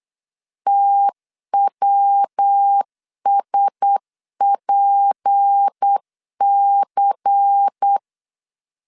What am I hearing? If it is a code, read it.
Morse code, heard as a keyed tone, about 8 wpm, 800 Hz. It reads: TWSPC
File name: LEZIONI / 6 PKX0 / 6 PKX 8wpm 3041